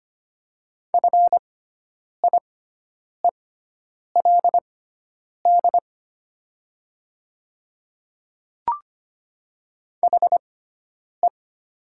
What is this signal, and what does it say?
Morse code, heard as a keyed tone, about 25 wpm, spaced as for 9 wpm, 700 Hz. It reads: FIELD HE
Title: 6x Character Spacing - Top 500 Words - 25wpm